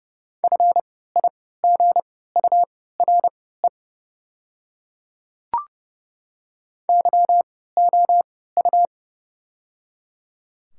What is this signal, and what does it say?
Morse code, heard as a keyed tone, about 30 wpm, spaced as for 17 wpm, 690 Hz. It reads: FIGURE YOU
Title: 3x Character Spacing - Top 500 Words - 30wpm